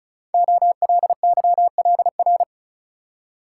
Morse code, heard as a keyed tone, about 36 wpm, 700 Hz. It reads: OLYLR
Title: Code Groups - 5 Letter:Numbers - 35wpm